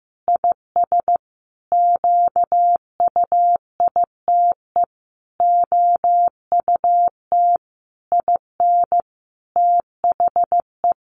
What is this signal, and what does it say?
Morse code, heard as a keyed tone, about 15 wpm, 700 Hz. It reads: IS QUITE OUT IN THE
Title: Sentences from Top 700 Words 15wpm